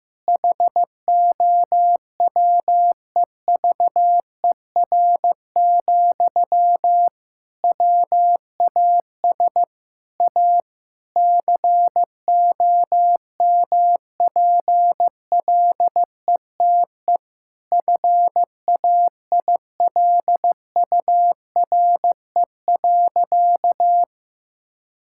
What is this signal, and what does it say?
Morse code, heard as a keyed tone, about 15 wpm, 700 Hz. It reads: HOWEVER, WAS A COMPLETE FAILURE.
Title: Sentences from Top 1000 Words - Review Part 6 - 15wpm